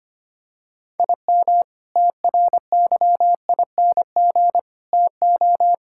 Morse code, heard as a keyed tone, about 25 wpm, 700 Hz. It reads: IM TRYING TO
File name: practice / 25wpm / Sentences with 4-Word Phrases - 25wpm